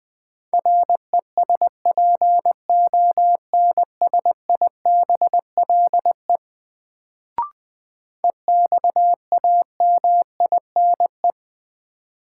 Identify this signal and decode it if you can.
Morse code, heard as a keyed tone, about 20 wpm, 700 Hz. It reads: RESPONSIBLE EXAMINE
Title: Sentences from Top 1600 Words - New Word Drill Alt - 20wpm